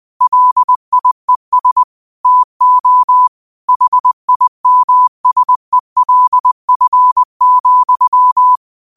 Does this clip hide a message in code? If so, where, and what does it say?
Yes: Morse code from 0.2 to 8.6 s, heard as a keyed tone, about 20 wpm, 1000 Hz. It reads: LIES TO HIMSELF,